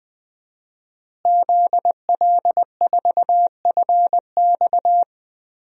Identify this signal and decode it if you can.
Morse code, heard as a keyed tone, about 21 wpm, 700 Hz. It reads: ZL4FX
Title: DXCC-100 Call Signs - 5 Characters - Repeated 3x - 20wpm